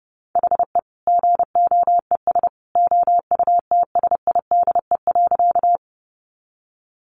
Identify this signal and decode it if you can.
Morse code, heard as a keyed tone, about 30 wpm, 700 Hz. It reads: HE GOES OUTSIDE.
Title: Easy Phrases 30wpm